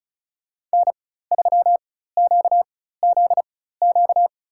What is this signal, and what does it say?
Morse code, heard as a keyed tone, about 36 wpm, spaced as for 16 wpm, 700 Hz. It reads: N3QZQ
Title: USA Call Signs - 4x Character Spacing - 35wpm